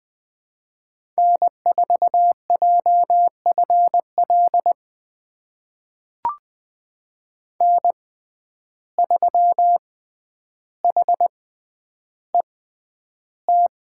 Morse code, heard as a keyed tone, about 20 wpm, spaced as for 10 wpm, 700 Hz. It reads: N4JFL N3HET
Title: USA Call Signs - 5 Characters - 6x Character Spacing - 20wpm